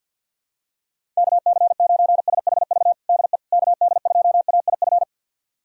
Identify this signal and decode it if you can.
Morse code, heard as a keyed tone, about 52 wpm, 700 Hz. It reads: KC9SHV DE KD2AIF
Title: USA Callsign DE Callsign - 6 Characters - 50wpm